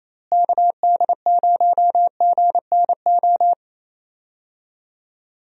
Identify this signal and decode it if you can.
Morse code, heard as a keyed tone, about 27 wpm, 700 Hz. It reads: KD0GNO
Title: USA Call Signs - 6 Characters - Repeated 2x - 28wpm